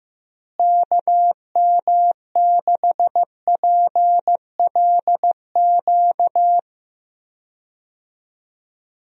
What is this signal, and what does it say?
Morse code, heard as a keyed tone, about 15 wpm, 700 Hz. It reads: KM6PLQ
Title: USA Call Signs - 6 Characters - Repeated 3x - 15wpm